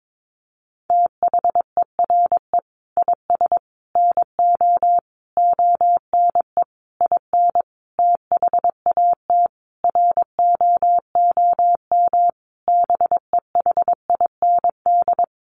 Morse code, heard as a keyed tone, about 22 wpm, 700 Hz. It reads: THERE IS NO ONE IN THAT ROOM BEHIND
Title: Sentences from Top 500 Words 22wpm